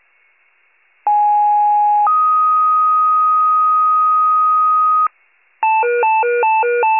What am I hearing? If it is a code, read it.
Two-tone page, heard as a one-second tone, then a three-second tone, about 830 then 1250 Hz. It is heard from 1.1 to 5.1 s.